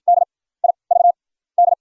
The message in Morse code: DIVD

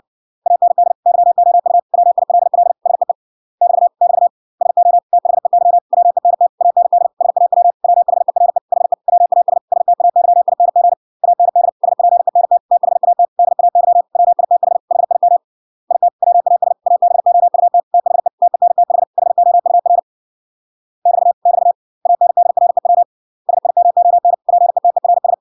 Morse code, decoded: AND YOU WILL SEE <BT> <BT> SO THEY WENT AND SAW WHERE HE WAS STAYING AND SPENT THAT DAY WITH HIM IT WAS ABOUT THE TENTH HOUR <BT> <BT> ANDREW SIMON PETERS

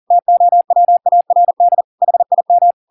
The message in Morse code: TOWARD HIM

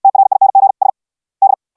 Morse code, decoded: TRING S D

